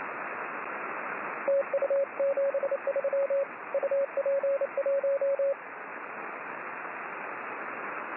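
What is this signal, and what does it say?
Morse code, heard as a keyed tone, about 29 wpm, 550 Hz, with radio noise. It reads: TU 73 UP1
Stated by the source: Morse signal on the 6 m band.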